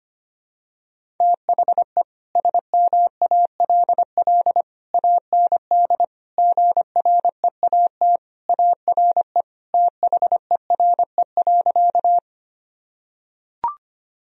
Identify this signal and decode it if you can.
Morse code, heard as a keyed tone, about 25 wpm, 700 Hz. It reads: THE SMALL AND GREAT ARE THERE.
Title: Sentences from Top 1000 Words - Review Part 1 - 25wpm